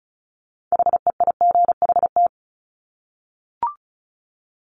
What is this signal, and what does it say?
Morse code, heard as a keyed tone, about 36 wpm, 700 Hz. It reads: HEIGHT
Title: Multi-letter Phonemes 35wpm